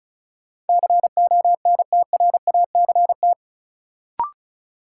Morse code, heard as a keyed tone, about 36 wpm, 700 Hz. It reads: CONTRACT
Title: Sentences from Top 1400 Words - New Word Drill Alt - 35wpm